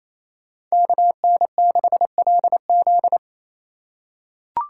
Morse code, heard as a keyed tone, about 28 wpm, 700 Hz. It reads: KN6LZ
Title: USA Call Signs - 5 Characters - 4x Character Spacing - 28wpm